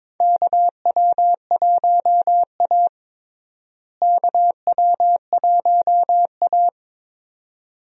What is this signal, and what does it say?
Morse code, heard as a keyed tone, about 22 wpm, 700 Hz. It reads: KW1A KW1A
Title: USA Call Signs - 4 Characters - Repeated 2x - 22wpm